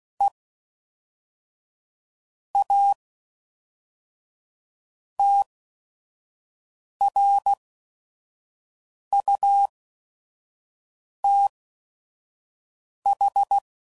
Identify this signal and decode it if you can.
Morse code, heard as a keyed tone, about 16 wpm, spaced as for 5 wpm, 800 Hz. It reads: E A TRUTH
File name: CW Code Groups @5WPM / Technical Subjects / AND Gate